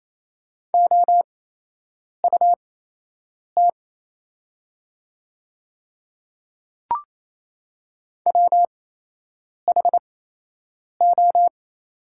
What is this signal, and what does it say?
Morse code, heard as a keyed tone, about 28 wpm, spaced as for 8 wpm, 700 Hz. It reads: OUT WHO
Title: 8x Character Spacing - Top 100 Words - 28wpm